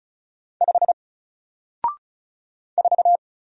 Morse code, heard as a keyed tone, about 34 wpm, 700 Hz. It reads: 5 4
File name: Single-Letter-Number-35wpm